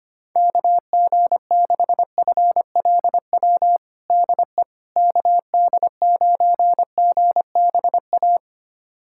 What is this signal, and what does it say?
Morse code, heard as a keyed tone, about 25 wpm, 700 Hz. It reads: KG6FLW DE KD9GBA
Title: USA Callsign DE Callsign 25wpm